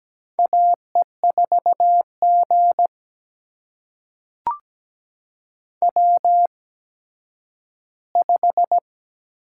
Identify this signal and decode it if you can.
Morse code, heard as a keyed tone, about 17 wpm, spaced as for 10 wpm, 700 Hz. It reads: AE4G W5
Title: USA Call Signs - 4 Characters - 8x Character Spacing - 17wpm